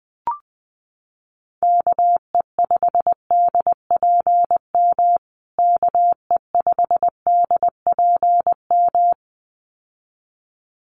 Morse code, heard as a keyed tone, about 20 wpm, 700 Hz. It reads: KE5DPM KE5DPM
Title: USA Call Signs Repeated 2x 20wpm